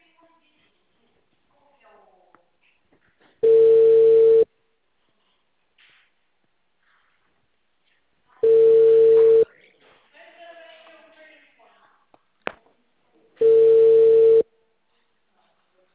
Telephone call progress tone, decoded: ringback tone

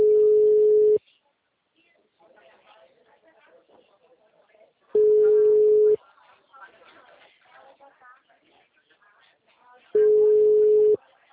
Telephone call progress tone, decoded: ringback tone

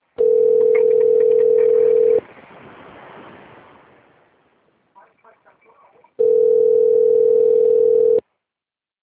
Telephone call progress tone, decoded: ringback tone

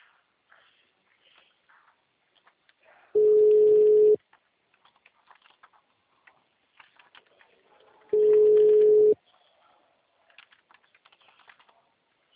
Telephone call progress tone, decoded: ringback tone

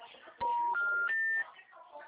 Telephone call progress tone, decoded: special information tone

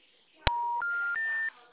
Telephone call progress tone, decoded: special information tone

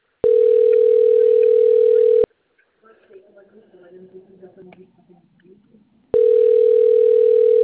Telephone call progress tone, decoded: ringback tone

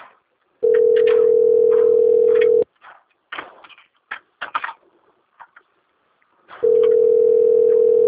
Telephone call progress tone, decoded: ringback tone

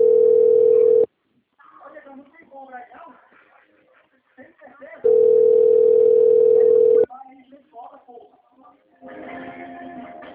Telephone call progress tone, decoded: ringback tone

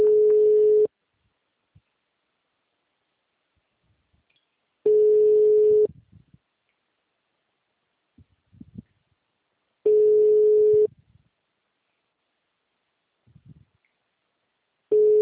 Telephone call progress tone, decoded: ringback tone

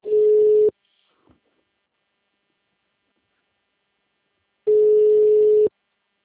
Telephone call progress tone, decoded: ringback tone

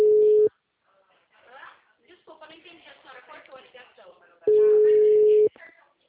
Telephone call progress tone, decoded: ringback tone

